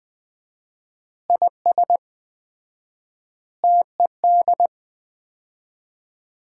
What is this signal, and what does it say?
Morse code, heard as a keyed tone, about 21 wpm, 700 Hz. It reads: IS TED
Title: Common QSO Phrases - 4x Word-Spacing - 20wpm